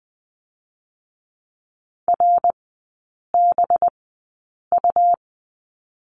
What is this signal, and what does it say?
Morse code, heard as a keyed tone, about 20 wpm, spaced as for 8 wpm, 700 Hz. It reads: RBU
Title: 184 Lesson 30e - Letter b - Three Characters 20wpm